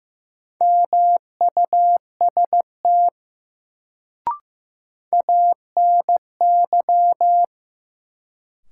Morse code, heard as a keyed tone, about 16 wpm, 700 Hz. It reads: MUST ANY
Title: Sentences from Top 200 Words - New Word Drill - 15wpm